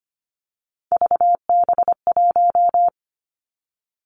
Morse code, heard as a keyed tone, about 25 wpm, 700 Hz. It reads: VB1
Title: Three Challenging Letter-Number Combinations 25wpm